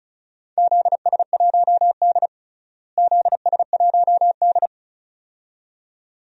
Morse code, heard as a keyed tone, about 36 wpm, 700 Hz. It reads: ZS1D ZS1D
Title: DXCC-100 Call Signs - 4 Characters - Repeated 2x - 35wpm